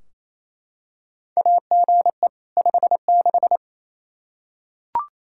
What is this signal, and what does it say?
Morse code, heard as a keyed tone, about 28 wpm, 700 Hz. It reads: AGE 56
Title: K1USN SST QSO Practice 28wpm